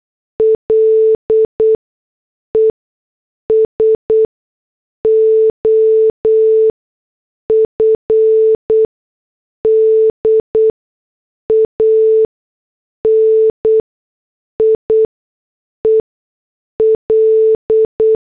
Morse code, heard as a keyed tone, about 8 wpm, 440 Hz. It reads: L E S O F D A N I E L
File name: morse / headlines